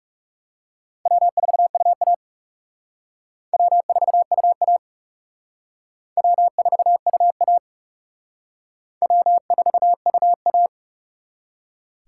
Morse code, heard as a keyed tone, about 36 wpm, 700 Hz. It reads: W4UA W4UA W4UA W4UA
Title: Speed Racing - USA 1x2 Call Signs 30wpm